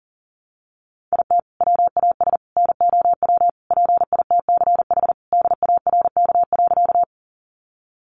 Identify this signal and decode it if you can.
Morse code, heard as a keyed tone, about 40 wpm, 700 Hz. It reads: IT WAS NOW PITCH DARK.
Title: Sentences from Top 1300 Words 40wpm